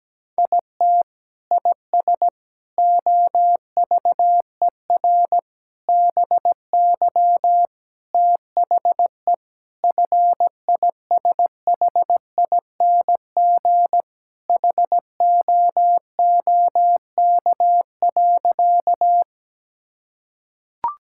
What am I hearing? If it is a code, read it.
Morse code, heard as a keyed tone, about 17 wpm, 700 Hz. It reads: IT IS OVER BY THE FISHING HOOK.
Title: Sentences from Top 1600 Words 17wpm